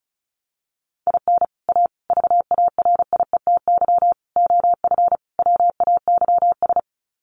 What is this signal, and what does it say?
Morse code, heard as a keyed tone, about 34 wpm, 700 Hz. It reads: IN A VARIETY OF WAYS
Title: Sets of 5 Words Encore 35wpm